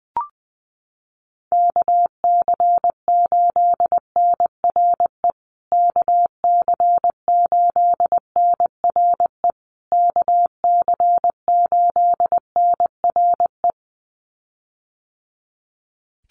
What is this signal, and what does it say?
Morse code, heard as a keyed tone, about 20 wpm, 700 Hz. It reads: KC8NRE KC8NRE KC8NRE
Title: USA Call Signs Repeated 3x 20wpm